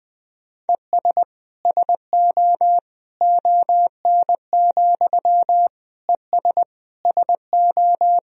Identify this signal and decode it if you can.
Morse code, heard as a keyed tone, about 20 wpm, 700 Hz. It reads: ES SO ON, ES SO
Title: Sentences from Top 100 Words and Abbreviations 20wpm